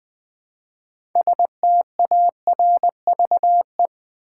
Morse code, heard as a keyed tone, about 20 wpm, 700 Hz. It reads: STARVE
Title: Build A Word - 3 Letters At A Time - 6-Letter Words - 20wpm